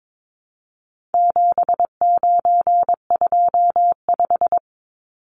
Morse code, heard as a keyed tone, about 22 wpm, 700 Hz. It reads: 7925
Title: Numbers - 4-Digit 22wpm